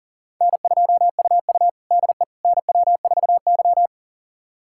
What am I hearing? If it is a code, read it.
Morse code, heard as a keyed tone, about 40 wpm, 690 Hz. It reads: N2UU DE NW4Y
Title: USA Callsign DE Callsign - 4 Characters - 40wpm